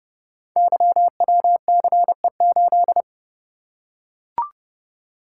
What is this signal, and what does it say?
Morse code, heard as a keyed tone, about 28 wpm, 700 Hz. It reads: YWCE8
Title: Code Groups - 5 Letter:Numbers - 30wpm